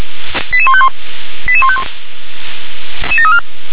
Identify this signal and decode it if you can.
Five-tone selective call (ZVEI), heard as quick runs of five tones, tones of about 70 ms. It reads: 88151 88151
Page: /357/SDR/ZVEI/88509